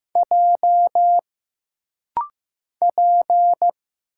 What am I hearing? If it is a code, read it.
Morse code, heard as a keyed tone, about 15 wpm, 700 Hz. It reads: J P
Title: Single Letter-Number Alt - Rapid-Fire 15wpm